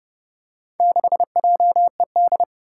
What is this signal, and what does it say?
Morse code, heard as a keyed tone, about 32 wpm, 700 Hz. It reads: 6JED
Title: Four Challenging Letter-Number Combinations 30wpm